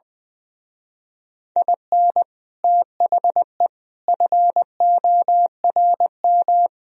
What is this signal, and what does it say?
Morse code, heard as a keyed tone, about 21 wpm, 700 Hz. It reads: IN THE FORM
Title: Speed-Racing - Sets of 4 Words 20wpm